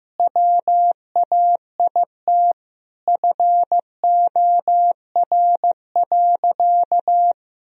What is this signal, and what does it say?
Morse code, heard as a keyed tone, about 15 wpm, 700 Hz. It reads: WAIT FOR.
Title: Sentences from Top 1000 Words - Review Part 3 - 15wpm